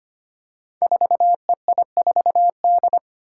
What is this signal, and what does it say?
Morse code, heard as a keyed tone, about 25 wpm, 700 Hz. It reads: 4EI4D
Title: Five Challenging Letter-Number Combinations 25wpm